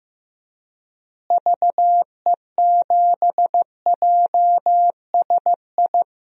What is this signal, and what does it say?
Morse code, heard as a keyed tone, about 15 wpm, 700 Hz. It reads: VE7JSI
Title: DXCC-100 Call Signs - 6 Characters - Repeated 3x - 15wpm